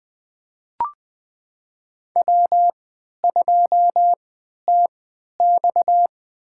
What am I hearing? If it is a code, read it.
Morse code, heard as a keyed tone, about 20 wpm, spaced as for 11 wpm, 700 Hz. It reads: W2TX
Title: USA Call Signs - 3x Character Spacing - 20wpm